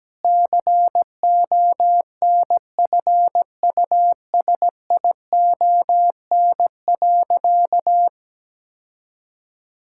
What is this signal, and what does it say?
Morse code, heard as a keyed tone, about 17 wpm, 700 Hz. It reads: CONFUSION.
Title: Sentences from Top 1600 Words 17wpm